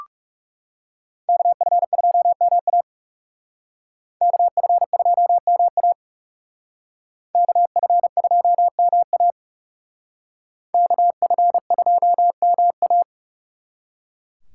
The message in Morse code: KF2MA KF2MA KF2MA KF2MA

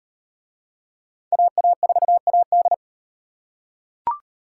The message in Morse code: AA4AD

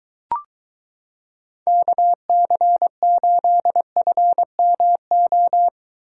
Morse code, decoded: KC8FMO

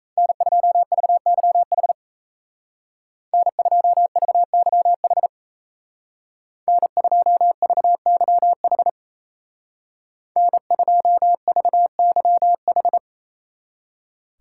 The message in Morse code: N2VYH N2VYH N2VYH N2VYH

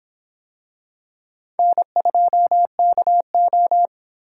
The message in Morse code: N2KO

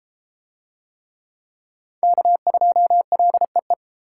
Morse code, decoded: K2LEE

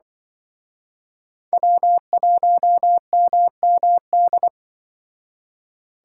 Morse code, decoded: W1MMD

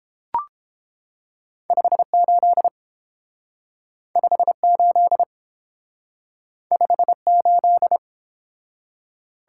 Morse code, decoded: 58 58 58

